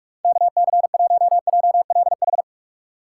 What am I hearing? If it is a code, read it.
Morse code, heard as a keyed tone, about 44 wpm, 700 Hz. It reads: KC1JLH
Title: USA Call Signs - 6 Characters - 2x Character Spacing - 45wpm